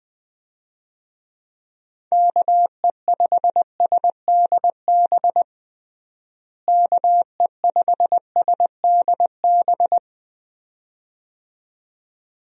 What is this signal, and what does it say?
Morse code, heard as a keyed tone, about 20 wpm, 700 Hz. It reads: KE5SDB KE5SDB
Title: USA Call Signs - 6 Characters - Repeated 2x - 20wpm